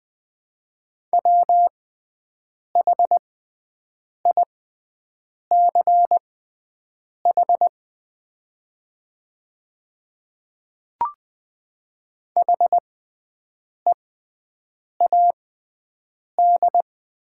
Morse code, heard as a keyed tone, about 20 wpm, spaced as for 7 wpm, 700 Hz. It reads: WHICH HEAD
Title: 6x Character Spacing - Top 500 Words - 20wpm